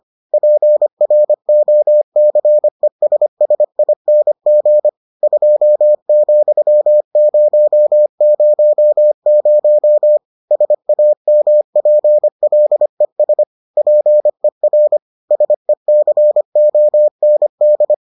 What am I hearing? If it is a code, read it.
Morse code, heard as a keyed tone, about 25 wpm, 600 Hz. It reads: PROCESSING 2,000 SAMPLES PER SECOND